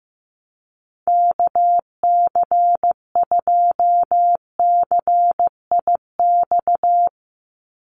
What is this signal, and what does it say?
Morse code, heard as a keyed tone, about 15 wpm, 700 Hz. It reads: KC2CIX
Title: USA Call Signs 15wpm